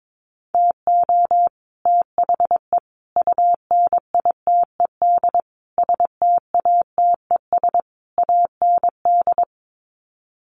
Morse code, heard as a keyed tone, about 22 wpm, 700 Hz. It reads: TO THE UNITED STATES AND
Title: Sets of 5 Words Encore 22wpm